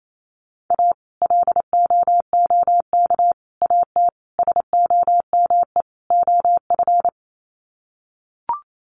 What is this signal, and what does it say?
Morse code, heard as a keyed tone, about 28 wpm, 700 Hz. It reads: A LOOK AT SOME OF
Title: Sets of 5 Words Encore2 28wpm